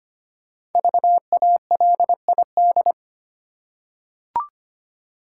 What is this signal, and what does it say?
Morse code, heard as a keyed tone, about 25 wpm, 700 Hz. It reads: VALID
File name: 5 Letter Words - First 3 Letters Given - 25wpm